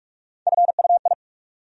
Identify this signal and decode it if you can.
Morse code, heard as a keyed tone, about 41 wpm, 700 Hz. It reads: FUI
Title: USA Call Sign Postfixes - 3 Letters 45wpm